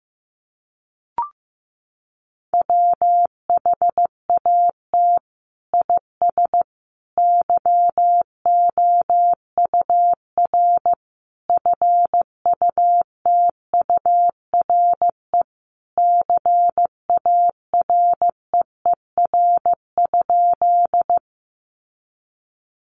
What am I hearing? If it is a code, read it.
Morse code, heard as a keyed tone, about 15 wpm, 700 Hz. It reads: WHAT IS YOUR FUTURE CAREER?
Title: Sentences from Top 1000 Words 15wpm